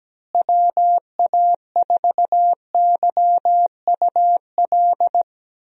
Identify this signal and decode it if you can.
Morse code, heard as a keyed tone, about 17 wpm, 700 Hz. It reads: WA4YUL
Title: USA Call Signs - 6 Characters - Repeated 3x - 17wpm